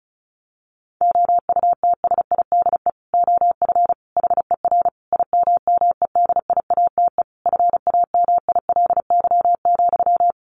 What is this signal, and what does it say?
Morse code, heard as a keyed tone, about 34 wpm, 700 Hz. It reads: OUTSIDE OF HER IMMEDIATE FAMILY,
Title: Sentences from Top 1200 Words 35wpm